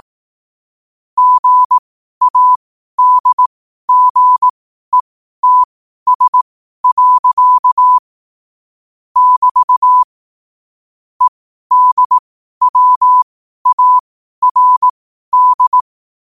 Morse code, heard as a keyed tone, about 18 wpm, spaced as for 12 wpm, 1000 Hz. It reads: GADGETS. <BT> EDWARD
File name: 12wpm